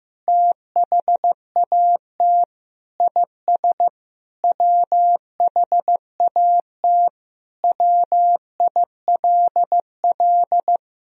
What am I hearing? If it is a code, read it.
Morse code, heard as a keyed tone, about 15 wpm, 700 Hz. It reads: THAT IS WHAT WILL
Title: Sentences from Top 500 Words and Abbreviations 15wpm